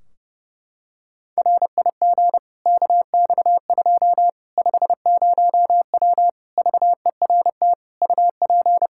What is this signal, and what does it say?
Morse code, heard as a keyed tone, about 29 wpm, 700 Hz. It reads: RIG KX2 50W VERT UP
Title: Practice QSO - Exchange 2 - Info - 30wpm